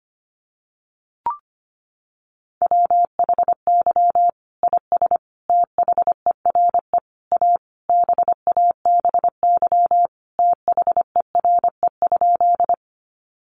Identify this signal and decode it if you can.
Morse code, heard as a keyed tone, about 25 wpm, 700 Hz. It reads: WHY IS THERE A BABY THERE?
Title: Sentences from Top 1300 Words 25wpm